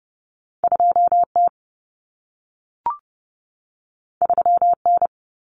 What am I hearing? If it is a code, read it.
Morse code, heard as a keyed tone, about 30 wpm, 700 Hz. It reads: 2T 3N
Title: Common Cut Numbers - 2-Digit 30wpm